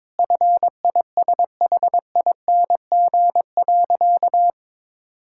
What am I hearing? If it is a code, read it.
Morse code, heard as a keyed tone, about 22 wpm, 700 Hz. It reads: FISHING.
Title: Sentences from Top 1000 Words - Review Part 5 - 22wpm